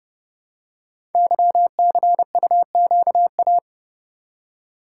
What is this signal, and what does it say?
Morse code, heard as a keyed tone, about 30 wpm, 700 Hz. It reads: YCUQA
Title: Code Groups - 5 Letters - 30wpm